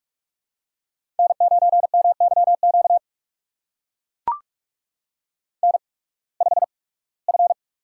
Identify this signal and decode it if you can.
Morse code, heard as a keyed tone, about 44 wpm, spaced as for 23 wpm, 700 Hz. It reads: N9MYQ N5F
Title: USA Call Signs - 5 Characters - 8x Character Spacing - 45wpm